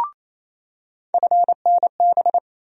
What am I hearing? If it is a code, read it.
Morse code, heard as a keyed tone, about 28 wpm, 700 Hz. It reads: FNB
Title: USA Call Sign Postfixes - 3 Letters 28wpm